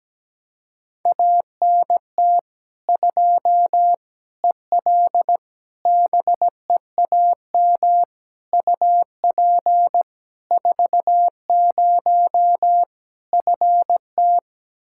Morse code, heard as a keyed tone, about 17 wpm, 700 Hz. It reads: ANT 2 EL BEAM UP 40 FT
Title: K1USN SST QSO Practice 17wpm